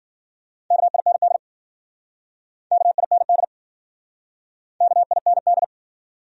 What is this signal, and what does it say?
Morse code, heard as a keyed tone, about 52 wpm, 700 Hz. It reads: KIND KIND KIND